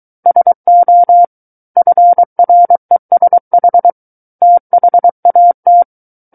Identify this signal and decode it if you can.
Morse code, heard as a keyed tone, about 23 wpm, 700 Hz. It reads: SO FRESH THAT
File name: Book One / 14-Chapter 14 - 23 WPM